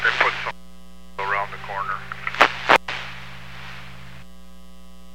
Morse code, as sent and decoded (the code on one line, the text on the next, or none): none